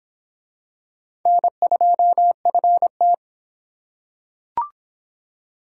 Morse code, heard as a keyed tone, about 23 wpm, 700 Hz. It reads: N2FT